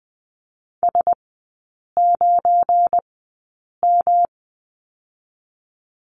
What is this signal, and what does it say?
Morse code, heard as a keyed tone, about 20 wpm, spaced as for 8 wpm, 700 Hz. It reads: S9M
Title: 184 Lesson 30e - Letter b - Three Characters 20wpm